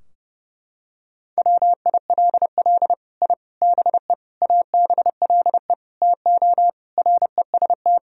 Morse code, heard as a keyed tone, about 32 wpm, 700 Hz. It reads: WILL I BE ABLE TO REST
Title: Sentences from Top 1000 Words - Review Part 3 - 30wpm